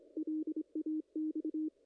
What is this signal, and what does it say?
Morse code, heard as a keyed tone, about 26 wpm, 310 Hz. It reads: LAX
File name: contests / ss-cw-k5tr-op-n5rz -0100